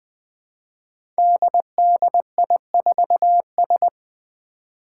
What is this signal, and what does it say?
Morse code, heard as a keyed tone, about 20 wpm, 700 Hz. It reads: DDI4S
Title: Five Challenging Letter-Number Combinations 20wpm